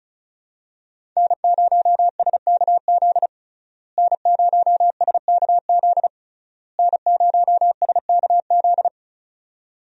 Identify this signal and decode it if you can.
Morse code, heard as a keyed tone, about 34 wpm, 700 Hz. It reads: N0SKZ N0SKZ N0SKZ
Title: USA Call Signs - 5 Characters - Repeated 3x - 35wpm